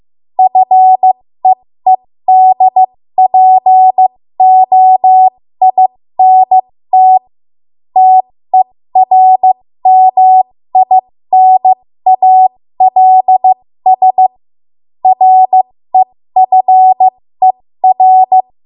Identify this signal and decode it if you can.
Morse code, heard as a keyed tone, about 15 wpm, spaced as for 13 wpm, 750 Hz. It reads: FEEDPOINT TERMINALS REFER
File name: Morse / 13WPM